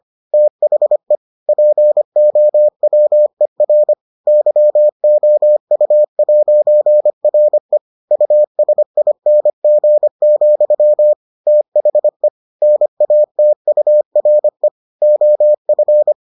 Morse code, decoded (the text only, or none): THE POWER YOU'RE USING, THE NATURE OF